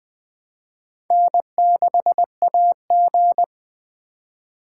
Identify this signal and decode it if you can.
Morse code, heard as a keyed tone, about 20 wpm, 700 Hz. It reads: N6AG
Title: USA Call Signs - 6x Character Spacing - 20wpm